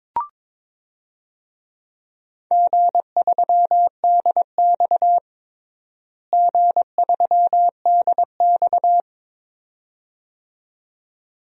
Morse code, heard as a keyed tone, about 22 wpm, 700 Hz. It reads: G3DX G3DX
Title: DXCC-100 Call Signs - 4 Characters - Repeated 2x - 22wpm